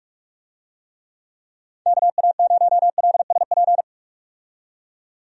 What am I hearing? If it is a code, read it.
Morse code, heard as a keyed tone, about 46 wpm, 700 Hz. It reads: KA0LSP
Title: USA Call Signs - 6 Characters - 45wpm